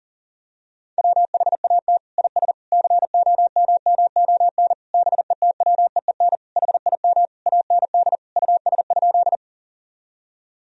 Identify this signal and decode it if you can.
Morse code, heard as a keyed tone, about 40 wpm, 700 Hz. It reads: WHAT IS COMMON BETWEEN HIM AND US?
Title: Sentences from Top 1000 Words - Review Part 3 - 40wpm